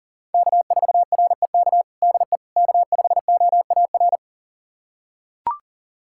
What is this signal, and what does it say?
Morse code, heard as a keyed tone, about 37 wpm, 700 Hz. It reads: K4REK DE K5OAR
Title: USA Callsign DE Callsign - 5 Characters - 40wpm